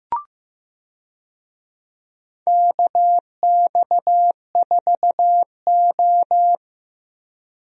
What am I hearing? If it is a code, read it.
Morse code, heard as a keyed tone, about 15 wpm, 700 Hz. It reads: KX4O
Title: USA Call Signs - 4 Characters - Repeated 2x - 15wpm